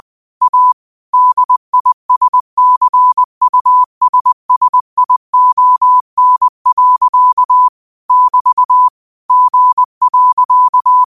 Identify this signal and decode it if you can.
Morse code, heard as a keyed tone, about 20 wpm, 1000 Hz. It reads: A DISCUSSION. <BT> G.
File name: qotdfiles / 20wpm